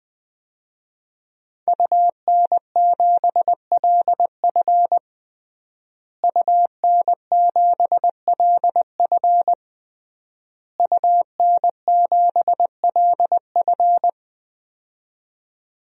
Morse code, heard as a keyed tone, about 20 wpm, 700 Hz. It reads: UN7LF UN7LF UN7LF
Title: DXCC-100 Call Signs - 5 Characters - Repeated 3x - 20wpm